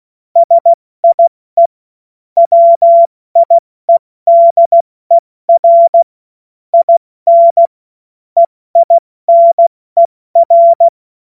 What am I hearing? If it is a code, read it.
Morse code, heard as a keyed tone, about 16 wpm, 680 Hz. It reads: SIE WIEDER IN EINER
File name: Morsekurs / Witze / Witz 02 April 2024 70 F80